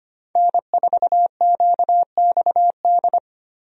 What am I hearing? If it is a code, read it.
Morse code, heard as a keyed tone, about 25 wpm, 700 Hz. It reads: N4QXD
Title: USA Call Signs - 5 Characters - 25wpm